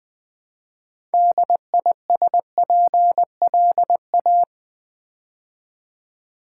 Morse code, heard as a keyed tone, about 20 wpm, 700 Hz. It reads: DISPLA
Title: Build A Word - 2 Letters At A Time - 8-Letter Words - 20wpm